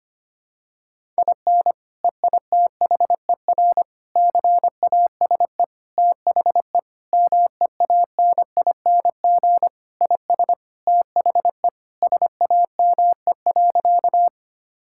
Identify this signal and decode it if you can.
Morse code, heard as a keyed tone, about 25 wpm, 700 Hz. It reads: IN EITHER CASE THE MEANING IS THE SAME.
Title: Sentences from Top 1000 Words - Review Part 6 - 25wpm